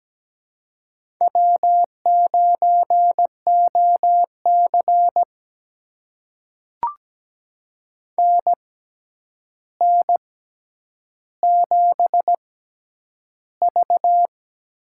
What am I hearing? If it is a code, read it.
Morse code, heard as a keyed tone, about 17 wpm, spaced as for 9 wpm, 700 Hz. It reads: W9OC NN7V
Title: USA Call Signs - 6x Character Spacing - 17wpm